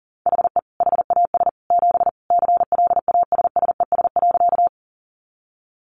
Morse code, heard as a keyed tone, about 40 wpm, 700 Hz. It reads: HE HAS 7 CLASSES.
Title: Additional Easy Phrases 40wpm